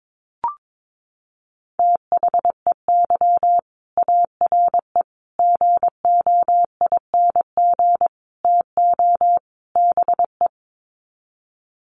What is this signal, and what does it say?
Morse code, heard as a keyed tone, about 22 wpm, 700 Hz. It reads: THEY ARE GOING TO BE
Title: Sets of 5 Words Encore2 22wpm